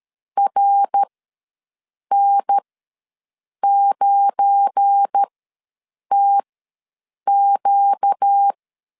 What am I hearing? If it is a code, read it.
Morse code, heard as a keyed tone, about 13 wpm, spaced as for 7 wpm, 800 Hz. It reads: RN9TQ
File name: LEZIONI / 7 GF9Z / 7 GF9Z ws 12wpm 3321